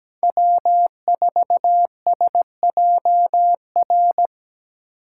Morse code, heard as a keyed tone, about 20 wpm, 700 Hz. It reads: W4SJR
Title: Speed Racing - USA Call Signs 17wpm